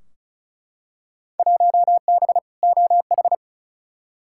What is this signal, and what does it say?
Morse code, heard as a keyed tone, about 36 wpm, 700 Hz. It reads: 1B OH
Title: Field Day Exchanges - 35wpm